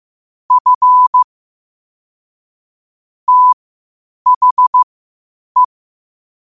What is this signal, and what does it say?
Morse code, heard as a keyed tone, about 15 wpm, spaced as for 8 wpm, 1000 Hz. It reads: F THE
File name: qotdfiles / 8wpm